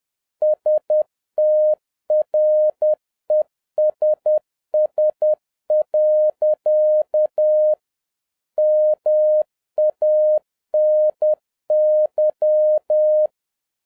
Morse code, heard as a keyed tone, about 10 wpm, 610 Hz. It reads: STRESS. MANY
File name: 2011-07-mp3 / n56 BACK FROM A BIG ACCIDENT 10wpm